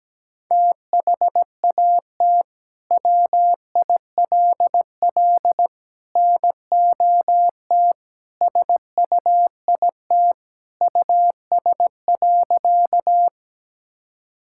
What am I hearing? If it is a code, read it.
Morse code, heard as a keyed tone, about 17 wpm, 700 Hz. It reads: THAT WILL NOT SUIT US.
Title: Sentences from Top 1600 Words 17wpm